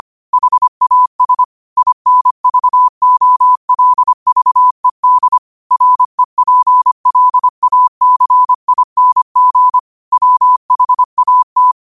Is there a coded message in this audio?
Yes: Morse code from 0.3 to 11.7 s, heard as a keyed tone, about 25 wpm, 1000 Hz. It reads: HAS INVOLVED REPLACING WHAT